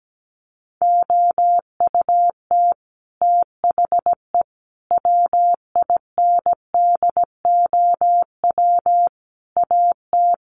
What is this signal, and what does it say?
Morse code, heard as a keyed tone, about 17 wpm, 700 Hz. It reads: OUT THE WINDOW AT
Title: Sets of 5 Words Encore2 17wpm